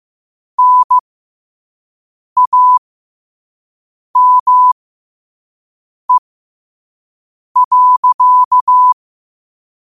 Morse code, heard as a keyed tone, about 15 wpm, spaced as for 5 wpm, 1000 Hz. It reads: NAME.
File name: qotdfiles / 5wpm